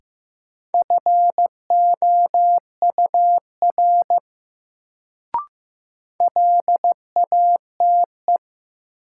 Morse code, heard as a keyed tone, about 15 wpm, 700 Hz. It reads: FOUR LATE
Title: Sentences from Top 300 Words - New Word Drill - 15wpm